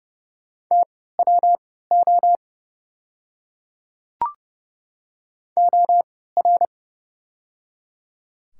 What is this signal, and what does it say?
Morse code, heard as a keyed tone, about 29 wpm, spaced as for 17 wpm, 690 Hz. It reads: TWO OR
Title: 3x Character Spacing - Top 100 Words - 30wpm